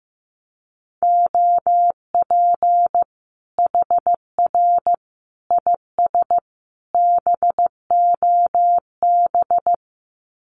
Morse code, heard as a keyed tone, about 15 wpm, 700 Hz. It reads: OP HR IS BOB
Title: Common QSO Phrases 15wpm